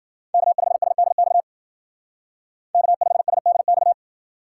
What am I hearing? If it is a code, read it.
Morse code, heard as a keyed tone, about 48 wpm, 700 Hz. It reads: K5SDX K5SDX
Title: Speed Racing - USA 1x3 Call Signs 40wpm